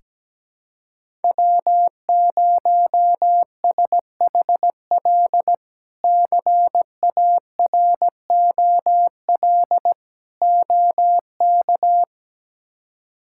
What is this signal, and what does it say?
Morse code, heard as a keyed tone, about 17 wpm, 700 Hz. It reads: W0SHL CAROL OK
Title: K1USN SST QSO Practice - 17wpm